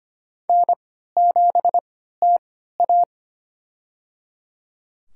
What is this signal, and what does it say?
Morse code, heard as a keyed tone, about 25 wpm, spaced as for 14 wpm, 700 Hz. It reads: N7TA
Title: USA Call Signs - 4 Characters - 3x Character Spacing - 25wpm